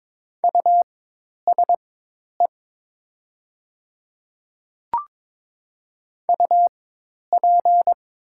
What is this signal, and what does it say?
Morse code, heard as a keyed tone, about 22 wpm, spaced as for 10 wpm, 700 Hz. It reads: USE UP
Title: 4x Character Spacing - Top 100 Words - 22wpm